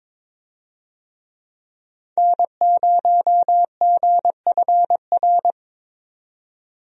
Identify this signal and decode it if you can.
Morse code, heard as a keyed tone, about 22 wpm, 700 Hz. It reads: N0GFR
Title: USA Call Signs - 5 Characters - 22wpm